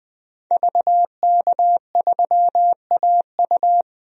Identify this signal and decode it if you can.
Morse code, heard as a keyed tone, about 20 wpm, 700 Hz. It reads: VK3AU
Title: DXCC-100 Call Signs - 5 Characters - 20wpm